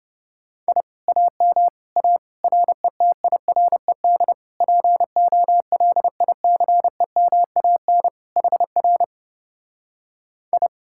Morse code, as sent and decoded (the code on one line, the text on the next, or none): .. / .- -- / .- / .-. - .. .-. -.. / .--. --- .-.. .. -.-. -- .- -. / .... .-. / ..
I AM A RETIRED POLICEMAN HR I